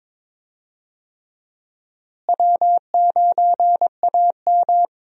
Morse code, heard as a keyed tone, about 22 wpm, 700 Hz. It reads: W9AM
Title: USA Call Signs - 4 Characters - Repeated 3x - 22wpm